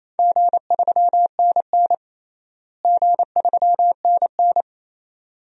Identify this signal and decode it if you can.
Morse code, heard as a keyed tone, about 28 wpm, 700 Hz. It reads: G3NN G3NN
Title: DXCC-100 Call Signs - 4 Characters - Repeated 2x - 28wpm